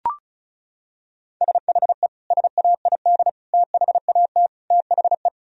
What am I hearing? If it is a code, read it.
Morse code, heard as a keyed tone, about 34 wpm, 700 Hz. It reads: SHE SAID THAT THE
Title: Sentences from Top 1000 Words - Review Part 4 - 35wpm